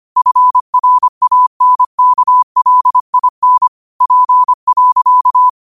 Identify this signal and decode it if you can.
Morse code, heard as a keyed tone, about 25 wpm, 1000 Hz. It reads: FRANKLIN P.